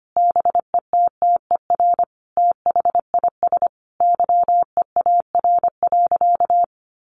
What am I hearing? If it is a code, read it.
Morse code, heard as a keyed tone, about 25 wpm, 700 Hz. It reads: BETTER THIS YEAR.
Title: Sentences from Top 1500 Words 25wpm